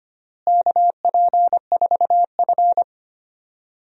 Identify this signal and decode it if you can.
Morse code, heard as a keyed tone, about 25 wpm, 700 Hz. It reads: KP4F
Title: DXCC-100 Call Signs - 4 Characters - Repeated 2x - 25wpm